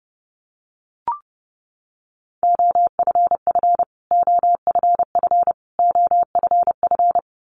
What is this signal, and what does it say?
Morse code, heard as a keyed tone, about 30 wpm, 700 Hz. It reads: OFF OFF OFF
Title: Top 500 Words Repeated 30wpm